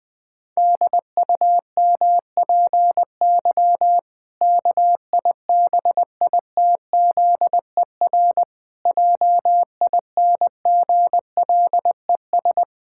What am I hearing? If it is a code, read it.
Morse code, heard as a keyed tone, about 19 wpm, 700 Hz. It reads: DUMPY KIBITZER JINGLES
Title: Pangram Challenge v2 - 20wpm